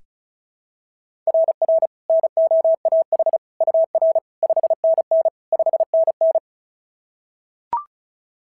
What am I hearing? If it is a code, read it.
Morse code, heard as a keyed tone, about 36 wpm, 690 Hz. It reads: RR NOAH UR 5NN 5NN
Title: SOTA QSO Practice 35wpm